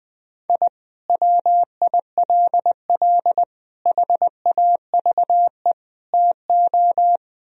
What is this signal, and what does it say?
Morse code, heard as a keyed tone, about 21 wpm, 700 Hz. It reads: I WILL HAVE TO
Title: Sentences from Top 1000 Words - Review Part 1 - 20wpm